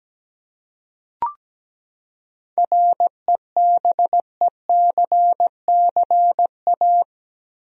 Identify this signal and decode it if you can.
Morse code, heard as a keyed tone, about 17 wpm, 700 Hz. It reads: REBECCA
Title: USA Common Names - 17wpm